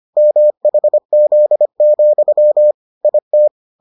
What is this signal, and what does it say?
Morse code, heard as a keyed tone, about 25 wpm, 600 Hz. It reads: MHZ, IT